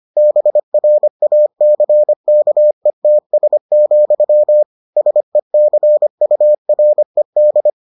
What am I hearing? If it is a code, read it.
Morse code, heard as a keyed tone, about 25 wpm, 600 Hz. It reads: BRACKETS, SECURED